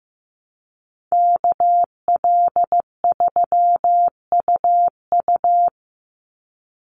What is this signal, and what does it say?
Morse code, heard as a keyed tone, about 15 wpm, 700 Hz. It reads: KL3UU
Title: USA Call Signs 15wpm